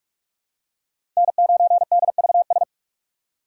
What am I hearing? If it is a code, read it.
Morse code, heard as a keyed tone, about 46 wpm, 700 Hz. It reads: N9DVS
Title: USA Call Signs - 5 Characters - 3x Character Spacing - 45wpm